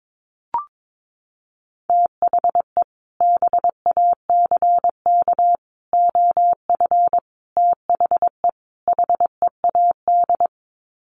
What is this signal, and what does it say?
Morse code, heard as a keyed tone, about 22 wpm, 700 Hz. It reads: THE BACK OF THE HEAD
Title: Sets of 5 Words Encore2 22wpm